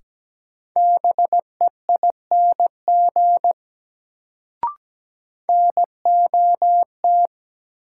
Morse code, heard as a keyed tone, about 17 wpm, 700 Hz. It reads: BEING NOT
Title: Sentences from Top 600 Words - New Word Drill - 17wpm